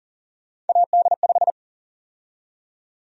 Morse code, heard as a keyed tone, about 40 wpm, 700 Hz. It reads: AD5
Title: USA Call Sign Prefixes - 40wpm